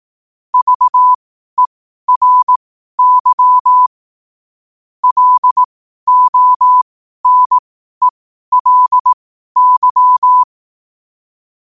Morse code, heard as a keyed tone, about 18 wpm, spaced as for 12 wpm, 1000 Hz. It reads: VERY LONELY
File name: qotdfiles / 12wpm